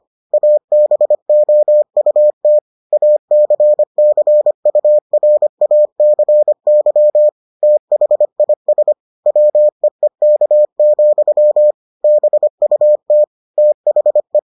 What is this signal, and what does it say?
Morse code, heard as a keyed tone, about 25 wpm, 600 Hz. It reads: ABOUT ACCURACY THIS WEEK, BUT THE